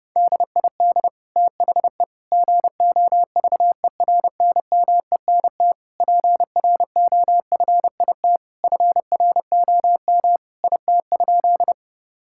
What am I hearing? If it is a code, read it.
Morse code, heard as a keyed tone, about 32 wpm, 700 Hz. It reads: DID THE GOVERNMENT PROFIT FROM IT?
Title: Sentences from Top 1000 Words - Review Part 5 - 30wpm